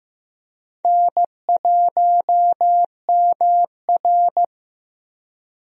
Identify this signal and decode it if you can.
Morse code, heard as a keyed tone, about 15 wpm, 700 Hz. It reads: N1MR
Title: USA Call Signs - 4 Characters - Repeated 2x - 15wpm